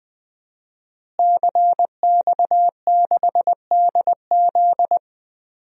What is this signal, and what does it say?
Morse code, heard as a keyed tone, about 21 wpm, 700 Hz. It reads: CX6DZ
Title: DXCC-100 Call Signs - 5 Characters - Repeated 3x - 20wpm